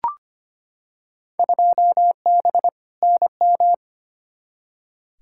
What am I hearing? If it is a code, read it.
Morse code, heard as a keyed tone, about 25 wpm, 700 Hz. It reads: 2B NM
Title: Field Day Exchanges Encore - 25wpm